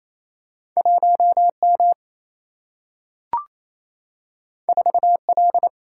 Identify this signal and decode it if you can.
Morse code, heard as a keyed tone, about 28 wpm, 700 Hz. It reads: 1M 4L
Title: Single-Letter Single-Number 28wpm